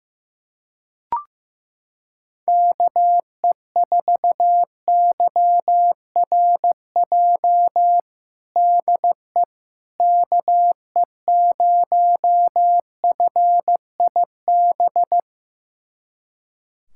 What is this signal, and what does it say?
Morse code, heard as a keyed tone, about 15 wpm, 700 Hz. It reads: KE4YRJ DE KE0FIB
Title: USA Callsign DE Callsign - 6 Characters - 15wpm